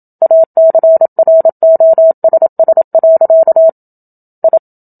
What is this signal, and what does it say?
Morse code, heard as a keyed tone, about 27 wpm, 650 Hz. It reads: ACROSS. I
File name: Book Two / 01-Chapter 1 - 27 WPM